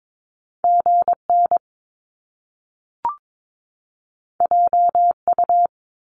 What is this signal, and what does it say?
Morse code, heard as a keyed tone, about 22 wpm, 700 Hz. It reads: GN JU
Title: Top Two Letter Combinations in Words 22wpm